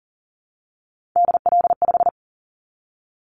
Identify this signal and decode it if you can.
Morse code, heard as a keyed tone, about 40 wpm, 700 Hz. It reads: DL5
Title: Callsign Prefixes for Top 25 DXCC - Repeated 3x - 40wpm